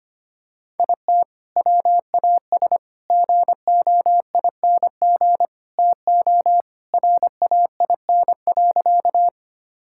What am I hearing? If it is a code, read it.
Morse code, heard as a keyed tone, about 25 wpm, 700 Hz. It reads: IT WAS GOING TO RAIN.
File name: practice / 25wpm / Sentences with 4-Word Phrases - 25wpm